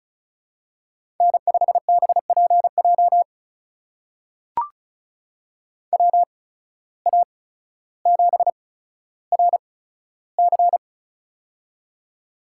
Morse code, heard as a keyed tone, about 36 wpm, spaced as for 15 wpm, 700 Hz. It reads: N5BPJ WA7RC
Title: USA Call Signs - 5 Characters - 8x Character Spacing - 35wpm